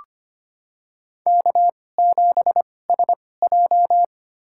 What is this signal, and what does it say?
Morse code, heard as a keyed tone, about 25 wpm, spaced as for 18 wpm, 700 Hz. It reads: K7SJ
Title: USA Call Signs - 4 Characters - 2x Character Spacing - 25wpm